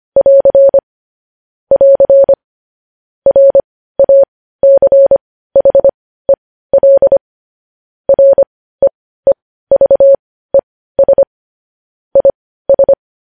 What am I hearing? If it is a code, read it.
Morse code, heard as a keyed tone, about 25 wpm, spaced as for 15 wpm, 550 Hz. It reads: <AR> <AR> RACHEL REEVES IS
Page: news headlines in morse code at 15 wpm